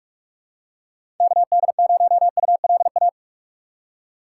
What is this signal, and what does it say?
Morse code, heard as a keyed tone, about 46 wpm, 700 Hz. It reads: KD0ULA
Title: USA Call Signs - 6 Characters - 6x Character Spacing - 45wpm